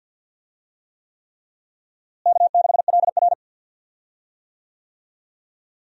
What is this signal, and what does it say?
Morse code, heard as a keyed tone, about 48 wpm, 700 Hz. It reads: K6LR
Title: USA Call Signs - 4 Characters - Repeated 3x - 50wpm